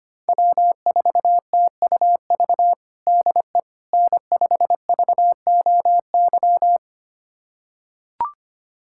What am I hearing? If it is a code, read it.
Morse code, heard as a keyed tone, about 25 wpm, 700 Hz. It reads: W4TUV DE N5VOY
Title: USA Callsign DE Callsign - 5 Characters - 25wpm